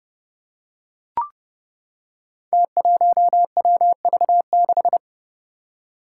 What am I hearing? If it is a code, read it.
Morse code, heard as a keyed tone, about 30 wpm, 700 Hz. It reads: T1WV6
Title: Five Challenging Letter-Number Combinations 30wpm